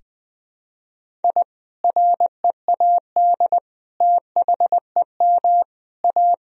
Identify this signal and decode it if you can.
Morse code, heard as a keyed tone, about 20 wpm, 700 Hz. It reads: I READ THEM A
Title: Sentences from Top 1000 Words - Review Part 1 - 20wpm